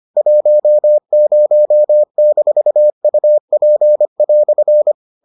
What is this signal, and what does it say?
Morse code, heard as a keyed tone, about 25 wpm, 600 Hz. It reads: 10-UP"